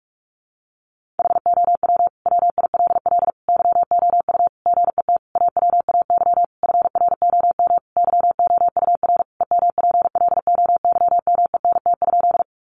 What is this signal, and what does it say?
Morse code, heard as a keyed tone, about 44 wpm, 700 Hz. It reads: HOW WILL YOU GET AWAY FROM YOUR EMPLOYMENT?
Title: Sentences from Top 1500 Words 45wpm